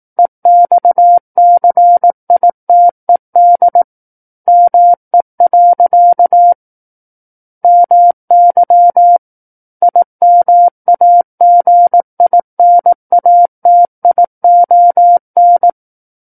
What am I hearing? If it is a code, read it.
Morse code, heard as a keyed tone, about 18 wpm, 700 Hz. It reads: EXCITED ME. MY IMAGINATION
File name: 18 WPM